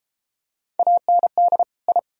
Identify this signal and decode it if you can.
Morse code, heard as a keyed tone, about 31 wpm, 700 Hz. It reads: AND I